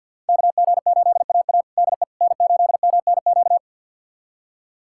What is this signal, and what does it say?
Morse code, heard as a keyed tone, about 46 wpm, 700 Hz. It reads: KG8AA DE N8MNQ
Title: USA Callsign DE Callsign - 5 Characters - 50wpm